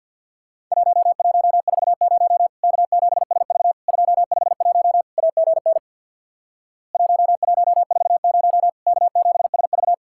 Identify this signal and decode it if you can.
Morse code, heard as a keyed tone, about 50 wpm, 700 Hz. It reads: 1140 K7SV 251 AGN 1140 K7SV
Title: Worked All Europe DX Contest - QTC Exchange with Repeats - 50wpm